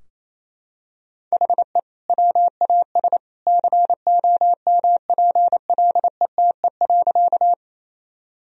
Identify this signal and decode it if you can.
Morse code, heard as a keyed tone, about 28 wpm, 700 Hz. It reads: HE WAS COMPLETE.
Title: Sentences from Top 1000 Words - Review Part 2 - 28wpm